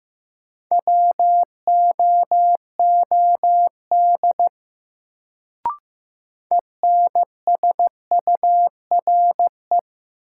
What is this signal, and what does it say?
Morse code, heard as a keyed tone, about 15 wpm, 700 Hz. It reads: WOOD ENSURE
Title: Sentences from Top 1100 Words - New Word Drill Alt - 15wpm